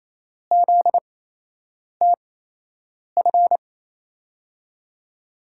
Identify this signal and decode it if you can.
Morse code, heard as a keyed tone, about 28 wpm, spaced as for 8 wpm, 700 Hz. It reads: ZTF
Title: USA Call Signs - 5 Characters - 8x Character Spacing - 28wpm